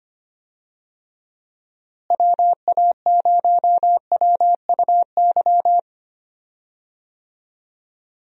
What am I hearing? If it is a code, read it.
Morse code, heard as a keyed tone, about 25 wpm, 700 Hz. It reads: WA0WUY
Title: USA Call Signs - 6 Characters - Repeated 3x - 25wpm